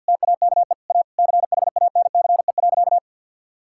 Morse code, heard as a keyed tone, about 48 wpm, 700 Hz. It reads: TAKE A CHANCE.
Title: Sentences from Top 1000 Words - Review Part 6 - 50wpm